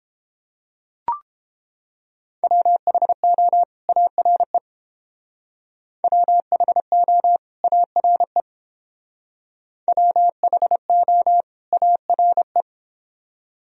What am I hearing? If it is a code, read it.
Morse code, heard as a keyed tone, about 30 wpm, 700 Hz. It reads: WHO ARE WHO ARE WHO ARE